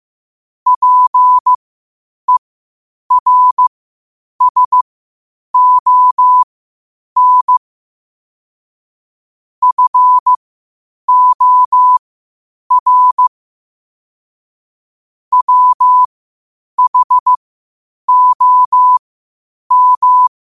Morse code, heard as a keyed tone, about 15 wpm, spaced as for 8 wpm, 1000 Hz. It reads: PERSON FOR WHOM
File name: qotdfiles / 8wpm